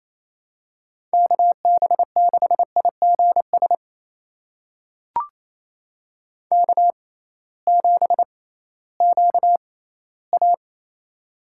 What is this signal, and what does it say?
Morse code, heard as a keyed tone, about 28 wpm, spaced as for 16 wpm, 700 Hz. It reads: KB6IGS K7QA
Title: USA Call Signs - 6x Character Spacing - 28wpm